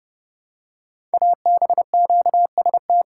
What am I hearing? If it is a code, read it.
Morse code, heard as a keyed tone, about 30 wpm, 700 Hz. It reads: ABQST
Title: Code Groups - 5 Letters - 30wpm